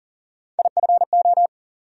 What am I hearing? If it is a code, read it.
Morse code, heard as a keyed tone, about 40 wpm, 700 Hz. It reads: IFO
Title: USA Call Sign Postfixes - 3 Letters 40wpm